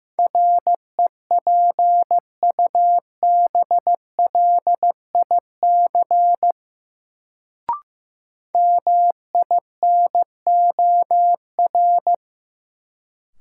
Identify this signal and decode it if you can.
Morse code, heard as a keyed tone, about 15 wpm, 700 Hz. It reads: REPUBLIC MINOR
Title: Sentences from Top 1500 Words - New Word Drill - 15wpm